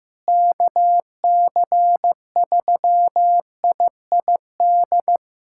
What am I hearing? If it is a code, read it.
Morse code, heard as a keyed tone, about 15 wpm, 700 Hz. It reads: KC3IID
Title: USA Callsign DE Callsign - 6 Characters - 15wpm